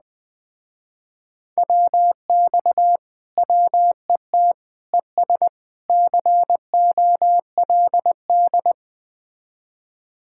Morse code, heard as a keyed tone, about 20 wpm, 700 Hz. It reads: WX WET ES COLD
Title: Speed-Racing Common QSO Phrases - 17wpm